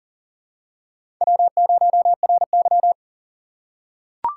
Morse code, heard as a keyed tone, about 40 wpm, 700 Hz. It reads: W0RY
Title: USA Call Signs - 3x Character Spacing - 40wpm